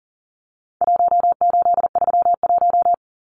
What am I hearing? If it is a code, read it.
Morse code, heard as a keyed tone, about 40 wpm, 700 Hz. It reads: 1831
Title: Numbers - Contest Style - 4-Digit 40wpm